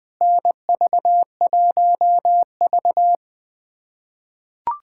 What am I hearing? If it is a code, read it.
Morse code, heard as a keyed tone, about 20 wpm, 700 Hz. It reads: NV1V
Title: USA Callsign DE Callsign - 4 Characters - 20wpm